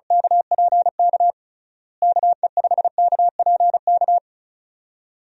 Morse code, heard as a keyed tone, about 36 wpm, 700 Hz. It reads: KPK KE5KPK
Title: USA Call Signs - 6 Characters - Repeated 2x - 35wpm